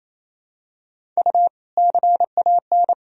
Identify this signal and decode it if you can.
Morse code, heard as a keyed tone, about 28 wpm, 700 Hz. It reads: U CAN
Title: Sentences from Top 100 Words and Abbreviations 28wpm